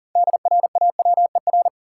Morse code, heard as a keyed tone, about 37 wpm, 700 Hz. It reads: DRAWER
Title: Six Letter Words - 40wpm